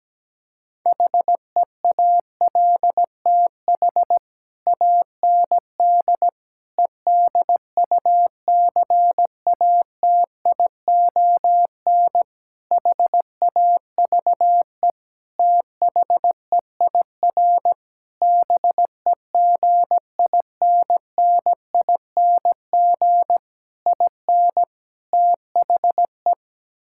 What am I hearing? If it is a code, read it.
Morse code, heard as a keyed tone, about 17 wpm, 700 Hz. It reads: HEALTH AND EDUCATION HAVE THEIR BEGINNING IN THE
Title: Sentences from Top 1000 Words - Review Part 6 - 17wpm